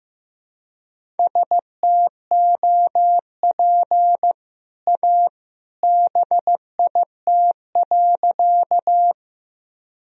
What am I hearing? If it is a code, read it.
Morse code, heard as a keyed tone, about 15 wpm, 700 Hz. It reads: STOP A BIT.
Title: Sentences from Top 1000 Words - Review Part 1 - 15wpm